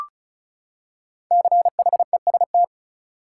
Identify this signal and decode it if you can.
Morse code, heard as a keyed tone, about 34 wpm, 700 Hz. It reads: CHEST
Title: Five Letter Words - 35wpm